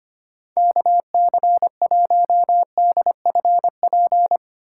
Morse code, heard as a keyed tone, about 31 wpm, 700 Hz. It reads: KC1DFP
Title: Speed Racing - USA 2x3 Call Signs 25wpm